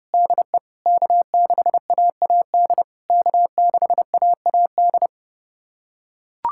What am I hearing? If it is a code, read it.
Morse code, heard as a keyed tone, about 30 wpm, 700 Hz. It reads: DE K6AAD K6AAD
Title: SOTA QSO Practice 30wpm